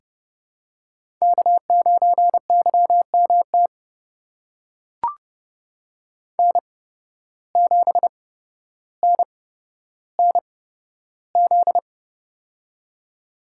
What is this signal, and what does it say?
Morse code, heard as a keyed tone, about 31 wpm, spaced as for 13 wpm, 700 Hz. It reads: K9YMT N7NNZ
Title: USA Call Signs - 5 Characters - 8x Character Spacing - 30wpm